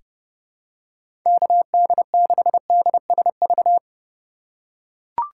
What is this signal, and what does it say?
Morse code, heard as a keyed tone, about 30 wpm, 700 Hz. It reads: KD6DSV
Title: USA Call Signs - 6 Characters - 2x Character Spacing - 30wpm